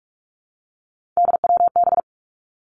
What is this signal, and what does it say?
Morse code, heard as a keyed tone, about 46 wpm, 700 Hz. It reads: DWB
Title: Three Challenging Letter-Number Combinations 45wpm